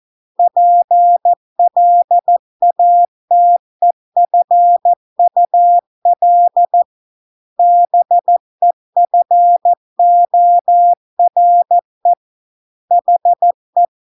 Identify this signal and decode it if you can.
Morse code, heard as a keyed tone, about 14 wpm, 700 Hz. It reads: PLATEFUL BEFORE HE